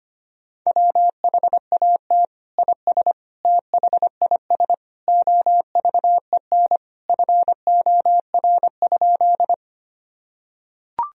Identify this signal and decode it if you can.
Morse code, heard as a keyed tone, about 25 wpm, 700 Hz. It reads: WHAT IS THIS OVEN FOR?
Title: Sentences from Top 1000 Words - Review Part 4 - 25wpm